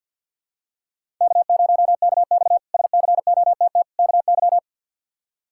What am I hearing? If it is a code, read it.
Morse code, heard as a keyed tone, about 49 wpm, 700 Hz. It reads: K0KX SCOTT KY
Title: K1USN SST QSO Practice - 50wpm